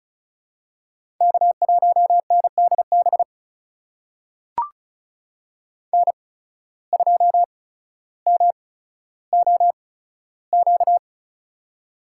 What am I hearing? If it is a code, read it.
Morse code, heard as a keyed tone, about 36 wpm, spaced as for 15 wpm, 700 Hz. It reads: K1NDB N2MOQ
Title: USA Call Signs - 5 Characters - 8x Character Spacing - 35wpm